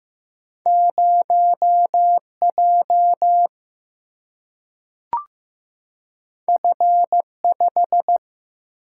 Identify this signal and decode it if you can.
Morse code, heard as a keyed tone, about 15 wpm, 700 Hz. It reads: 0 J F 5
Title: Single-Letter Single-Number 15wpm